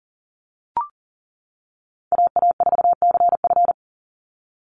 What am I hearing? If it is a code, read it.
Morse code, heard as a keyed tone, about 40 wpm, 700 Hz. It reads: AA4CF
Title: Call Signs - 40wpm